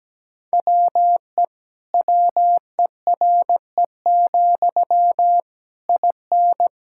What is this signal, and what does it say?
Morse code, heard as a keyed tone, about 17 wpm, 700 Hz. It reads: WE WERE, IN
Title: Sentences from Top 1000 Words - Review Part 5 - 17wpm